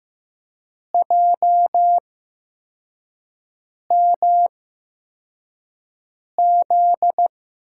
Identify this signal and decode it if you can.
Morse code, heard as a keyed tone, about 15 wpm, spaced as for 4 wpm, 700 Hz. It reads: JMZ
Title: USA Call Signs - 6 Characters - 8x Character Spacing - 15wpm